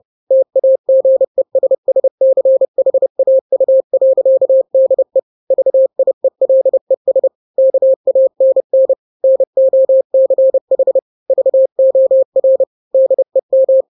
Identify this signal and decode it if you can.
Morse code, heard as a keyed tone, about 30 wpm, 530 Hz. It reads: TAGESSCHAU.DE VIELES KANN NOCH VOR DEM
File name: daily / 29wpm